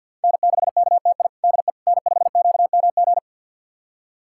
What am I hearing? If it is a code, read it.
Morse code, heard as a keyed tone, about 48 wpm, 700 Hz. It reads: N6KTI DE N5QMG
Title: USA Callsign DE Callsign - 5 Characters - 50wpm